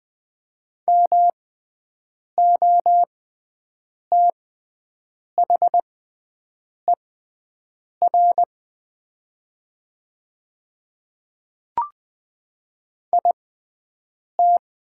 Morse code, heard as a keyed tone, about 20 wpm, spaced as for 7 wpm, 700 Hz. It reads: MOTHER IT
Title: 6x Character Spacing - Top 200 Words - 20wpm